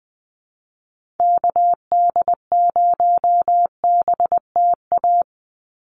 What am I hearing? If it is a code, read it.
Morse code, heard as a keyed tone, about 20 wpm, 700 Hz. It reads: KD0BTA
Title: USA Call Signs 20wpm